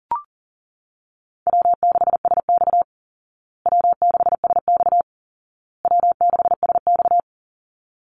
Morse code, heard as a keyed tone, about 40 wpm, 700 Hz. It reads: W6SX W6SX W6SX
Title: Easy Call Signs Repeated 3x - 40wpm